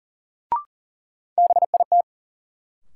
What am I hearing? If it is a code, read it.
Morse code, heard as a keyed tone, about 40 wpm, 700 Hz. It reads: BIT
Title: Sentences from Top 900 Words - New Word Drill - 40wpm